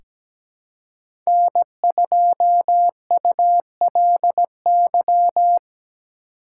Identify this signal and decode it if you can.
Morse code, heard as a keyed tone, about 17 wpm, 700 Hz. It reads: N2ULY
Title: USA Call Signs - 5 Characters - 2x Character Spacing - 17wpm